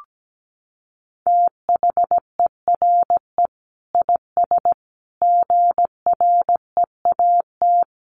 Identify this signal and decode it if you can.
Morse code, heard as a keyed tone, about 17 wpm, 700 Hz. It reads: THERE IS GREAT
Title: Sentences from Top 1200 Words 17wpm